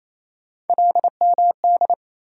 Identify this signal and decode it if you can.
Morse code, heard as a keyed tone, about 28 wpm, 700 Hz. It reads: LMD
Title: USA Call Sign Postfixes - 3 Letters 28wpm